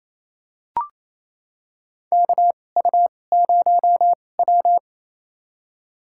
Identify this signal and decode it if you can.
Morse code, heard as a keyed tone, about 28 wpm, spaced as for 20 wpm, 700 Hz. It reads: KU0W
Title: USA Call Signs - 4 Characters - 2x Character Spacing - 28wpm